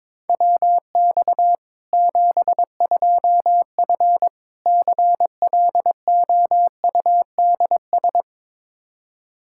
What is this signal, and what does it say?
Morse code, heard as a keyed tone, about 22 wpm, 700 Hz. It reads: WX 72F CLOUDS
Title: K1USN SST QSO Practice 22wpm